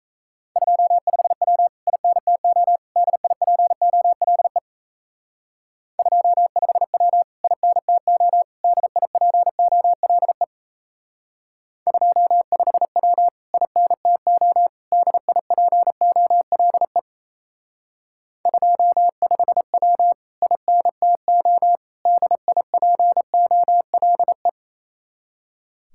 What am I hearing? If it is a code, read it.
Morse code, heard as a keyed tone, about 36 wpm, 700 Hz. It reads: 25W INTO DIPOLE 25W INTO DIPOLE 25W INTO DIPOLE 25W INTO DIPOLE